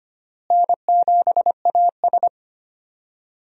Morse code, heard as a keyed tone, about 25 wpm, 700 Hz. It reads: N7AS
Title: USA Call Signs - 4 Characters - Repeated 3x - 25wpm